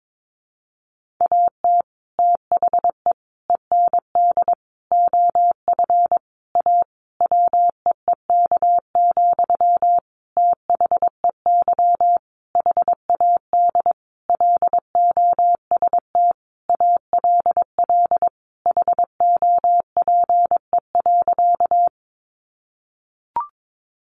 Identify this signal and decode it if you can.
Morse code, heard as a keyed tone, about 22 wpm, 700 Hz. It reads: AT THE END OF A WEEK, THEY HAD LOST ALL HOPE.
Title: Sentences from Top 1400 Words 22wpm